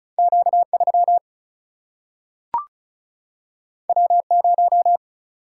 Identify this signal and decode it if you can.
Morse code, heard as a keyed tone, about 36 wpm, 700 Hz. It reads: Q 3 W 0
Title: Single-Letter Single-Number 35wpm